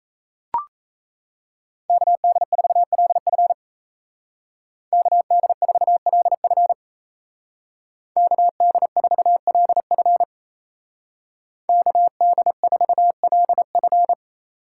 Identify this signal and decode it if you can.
Morse code, heard as a keyed tone, about 36 wpm, 700 Hz. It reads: KD4LF KD4LF KD4LF KD4LF